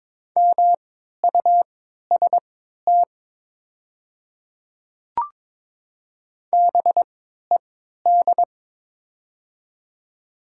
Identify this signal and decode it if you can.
Morse code, heard as a keyed tone, about 22 wpm, spaced as for 12 wpm, 700 Hz. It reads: MUST BED
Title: 3x Character Spacing - Top 500 Words - 22wpm